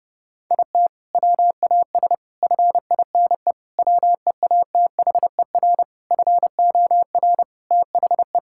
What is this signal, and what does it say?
Morse code, heard as a keyed tone, about 30 wpm, 700 Hz. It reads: IT WAS FINE WEATHER FOR THE
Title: Sentences from Top 1000 Words - Review Part 5 - 30wpm